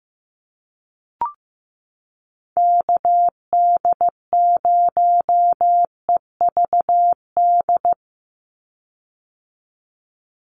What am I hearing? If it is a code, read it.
Morse code, heard as a keyed tone, about 15 wpm, 700 Hz. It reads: KD0EVD
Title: USA Call Signs 15wpm